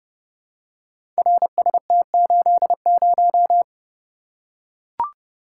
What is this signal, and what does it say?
Morse code, heard as a keyed tone, about 31 wpm, 700 Hz. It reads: RST80
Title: Code Groups - 5 Letter:Numbers - 30wpm